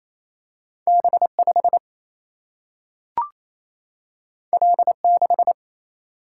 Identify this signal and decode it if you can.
Morse code, heard as a keyed tone, about 28 wpm, 700 Hz. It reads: B5 L6
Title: Single-Letter Single-Number 28wpm